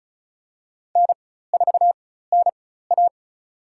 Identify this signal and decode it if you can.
Morse code, heard as a keyed tone, about 34 wpm, spaced as for 16 wpm, 700 Hz. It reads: N4NA
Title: USA Call Signs - 4 Characters - 4x Character Spacing - 35wpm